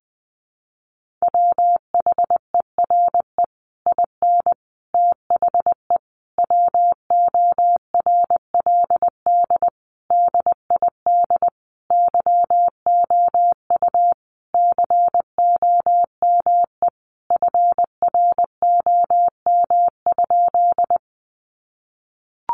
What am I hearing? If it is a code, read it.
Morse code, heard as a keyed tone, about 20 wpm, 700 Hz. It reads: WHERE IN THE WORLD DID YOU COME FROM?
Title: Sentences from Top 200 Words 20wpm